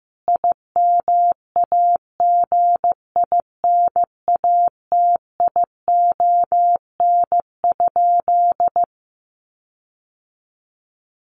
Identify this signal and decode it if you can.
Morse code, heard as a keyed tone, about 15 wpm, 700 Hz. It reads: IMAGINATION?
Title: Sentences from Top 1300 Words 15wpm